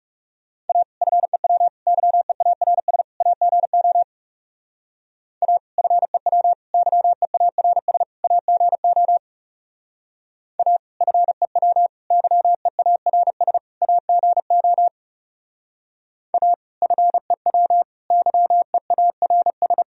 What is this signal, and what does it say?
Morse code, heard as a keyed tone, about 36 wpm, 700 Hz. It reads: A FEW YEARS AGO A FEW YEARS AGO A FEW YEARS AGO A FEW YEARS
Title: Speed-Racing - Sets of 4 Words 30wpm